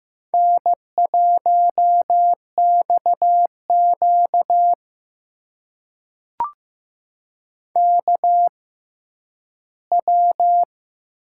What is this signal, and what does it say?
Morse code, heard as a keyed tone, about 15 wpm, spaced as for 10 wpm, 700 Hz. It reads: N1XQ KW
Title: USA Call Signs - 4 Characters - 6x Character Spacing - 15wpm